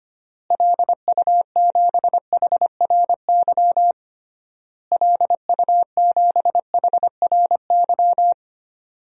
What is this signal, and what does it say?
Morse code, heard as a keyed tone, about 25 wpm, 700 Hz. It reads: LU7HRY LU7HRY
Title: DXCC-100 Call Signs - 6 Characters - Repeated 3x - 25wpm